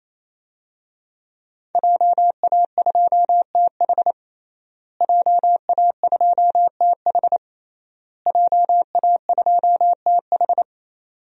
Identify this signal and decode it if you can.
Morse code, heard as a keyed tone, about 28 wpm, 700 Hz. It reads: JA2TH JA2TH JA2TH
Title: DXCC-100 Call Signs - 5 Characters - Repeated 3x - 28wpm